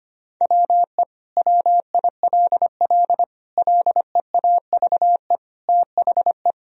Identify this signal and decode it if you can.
Morse code, heard as a keyed tone, about 25 wpm, 700 Hz. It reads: WE WILL LEAVE THE
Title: Sentences from Top 1000 Words - Review Part 2 - 25wpm